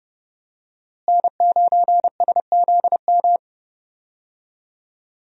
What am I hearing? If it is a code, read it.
Morse code, heard as a keyed tone, about 30 wpm, 700 Hz. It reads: N9SZM
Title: USA Call Signs - 5 Characters - Repeated 3x - 30wpm